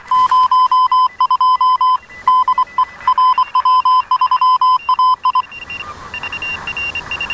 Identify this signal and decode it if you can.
Morse code, heard as a keyed tone, about 23 wpm, 1020 Hz. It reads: 02 DE RW3AI
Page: С удовольствием поработал в тесте "Партизанский радист", QRP/p.